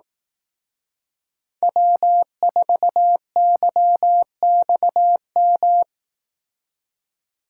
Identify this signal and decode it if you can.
Morse code, heard as a keyed tone, about 18 wpm, 700 Hz. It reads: W4YXM
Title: Speed Racing - USA 1x3 Call Signs 15wpm